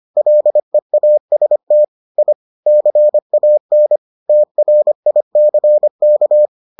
Morse code, heard as a keyed tone, about 25 wpm, 600 Hz. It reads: LEAST I CAN TRICK